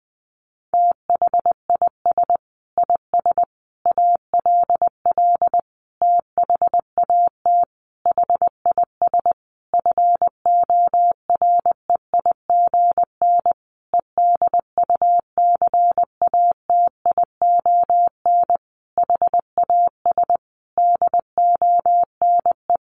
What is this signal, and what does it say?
Morse code, heard as a keyed tone, about 19 wpm, 700 Hz. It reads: THIS IS ALL THAT HIS FOREIGN EDUCATION HAS DONE
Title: Sentences from Top 1400 Words 20wpm